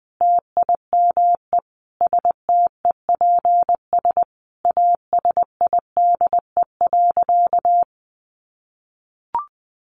Morse code, heard as a keyed tone, about 20 wpm, 700 Hz. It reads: TIME STEPS ASIDE.
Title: A Sound Of Thunder 20wpm